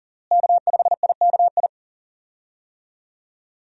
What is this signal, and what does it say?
Morse code, heard as a keyed tone, about 40 wpm, 700 Hz. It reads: K5IKI
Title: USA Call Signs - 5 Characters - Repeated 3x - 40wpm